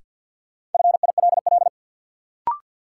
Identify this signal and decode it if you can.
Morse code, heard as a keyed tone, about 48 wpm, 700 Hz. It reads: FILL
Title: Sentences from Top 1000 Words - Review Part 3 - New Word Drill - 50wpm